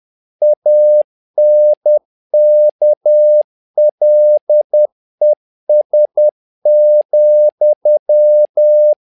Morse code, heard as a keyed tone, about 10 wpm, 600 Hz. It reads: ANKLES,
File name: learning center / tower trouble / tower trouble 10wpm